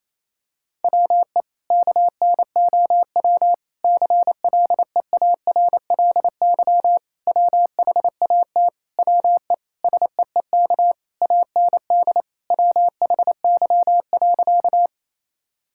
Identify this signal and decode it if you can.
Morse code, heard as a keyed tone, about 28 wpm, 700 Hz. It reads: WE KNOW CLEARLY WHAT WE SEEK AND WHY.
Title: Sentences from Top 1600 Words 28wpm